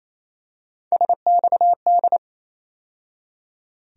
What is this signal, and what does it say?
Morse code, heard as a keyed tone, about 28 wpm, 700 Hz. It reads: SXD
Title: USA Call Sign Postfixes - 3 Letters 28wpm